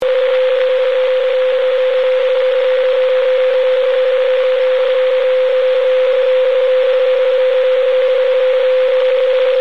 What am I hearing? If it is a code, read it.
PSK31, about 31.25 baud.